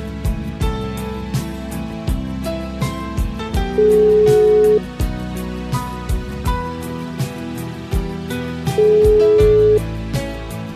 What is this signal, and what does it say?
Ringback tone (Europe, 425 Hz), with noise.